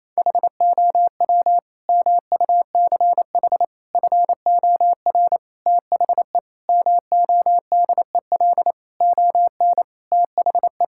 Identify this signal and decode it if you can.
Morse code, heard as a keyed tone, about 28 wpm, 700 Hz. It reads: HOW MUCH FOR THE MODEL ON THE
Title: Sentences from Top 1000 Words - Review Part 5 - 28wpm